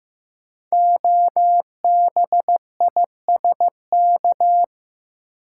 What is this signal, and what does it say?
Morse code, heard as a keyed tone, about 15 wpm, 700 Hz. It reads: OBISK
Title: Code Groups - 5 Letters - 15wpm